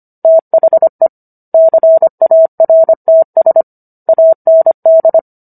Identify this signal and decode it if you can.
Morse code, heard as a keyed tone, about 25 wpm, 650 Hz. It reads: THE CARTS AND